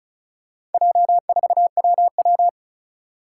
Morse code, heard as a keyed tone, about 34 wpm, 700 Hz. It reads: J4WW
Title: Four Challenging Letter-Number Combinations 35wpm